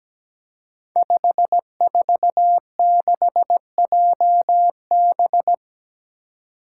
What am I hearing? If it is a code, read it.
Morse code, heard as a keyed tone, about 17 wpm, 700 Hz. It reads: 546JB
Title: Five Challenging Letter-Number Combinations 17wpm